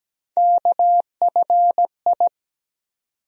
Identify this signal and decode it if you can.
Morse code, heard as a keyed tone, about 17 wpm, 700 Hz. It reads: KFI
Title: USA Call Sign Postfixes - 3 Letters 17wpm